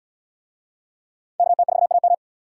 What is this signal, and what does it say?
Morse code, heard as a keyed tone, about 70 wpm, 700 Hz. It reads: KT4AM